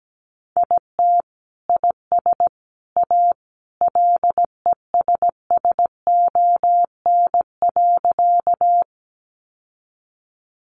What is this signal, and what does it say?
Morse code, heard as a keyed tone, about 17 wpm, 700 Hz. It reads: IT IS A LESSON.
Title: Sentences from Top 1200 Words 17wpm